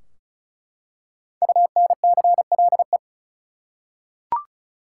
Morse code, heard as a keyed tone, about 36 wpm, 700 Hz. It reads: UNCLE
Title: Five Letter Words - 35wpm